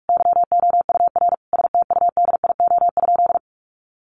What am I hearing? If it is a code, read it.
Morse code, heard as a keyed tone, about 44 wpm, 700 Hz. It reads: YOUR STUDIO?
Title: Sentences from Top 1200 Words 45wpm